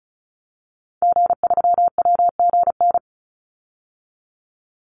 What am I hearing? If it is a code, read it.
Morse code, heard as a keyed tone, about 34 wpm, 700 Hz. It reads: G3WGN
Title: Call Signs - 35wpm